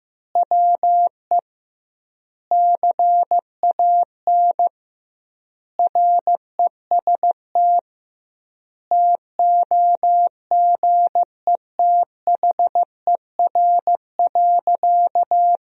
Morse code, heard as a keyed tone, about 15 wpm, 700 Hz. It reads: WE CAN REST TOGETHER.
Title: 2x Word-Spacing - Sentences from Top 500 Words - 15wpm